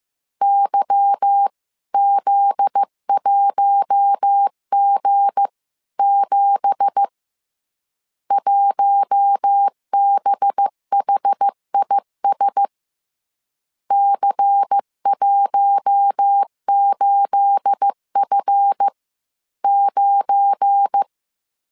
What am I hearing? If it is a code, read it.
Morse code, heard as a keyed tone, about 15 wpm, spaced as for 13 wpm, 800 Hz. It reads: YZ1G7 1BHIS C18F9
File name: CW / DET / 1 9 1500C S A 14wpm 3222